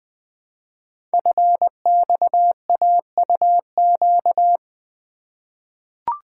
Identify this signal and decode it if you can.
Morse code, heard as a keyed tone, about 20 wpm, 700 Hz. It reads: FXAUQ
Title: Code Groups - 5 Letters - 20wpm